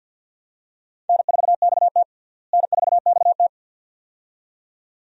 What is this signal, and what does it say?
Morse code, heard as a keyed tone, about 50 wpm, 700 Hz. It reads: N4XT N4XT
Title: USA Call Signs - 4 Characters - Repeated 2x - 50wpm